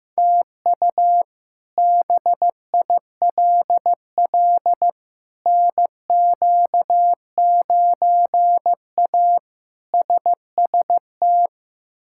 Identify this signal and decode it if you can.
Morse code, heard as a keyed tone, about 15 wpm, 700 Hz. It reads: TU BILL NQ9A SST
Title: K1USN SST QSO Practice - 15wpm